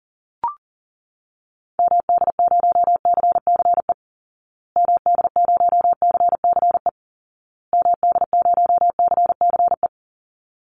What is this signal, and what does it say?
Morse code, heard as a keyed tone, about 37 wpm, 700 Hz. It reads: MD0CCE MD0CCE MD0CCE
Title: Call Signs Repeated 2x - 40wpm